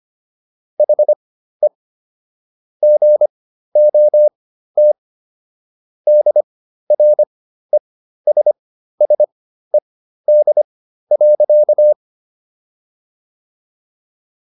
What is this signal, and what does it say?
Morse code, heard as a keyed tone, about 25 wpm, spaced as for 13 wpm, 600 Hz. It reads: HE GOT DRESSED.